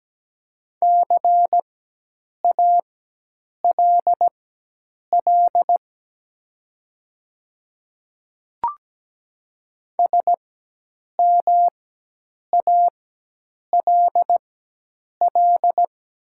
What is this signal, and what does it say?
Morse code, heard as a keyed tone, about 17 wpm, spaced as for 8 wpm, 700 Hz. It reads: CALL SMALL
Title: 4x Character Spacing - Top 200 Words - 17wpm